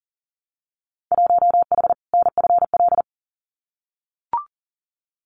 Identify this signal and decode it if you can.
Morse code, heard as a keyed tone, about 40 wpm, 700 Hz. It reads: 1H NFL
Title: Winter Field Day Exchanges 40wpm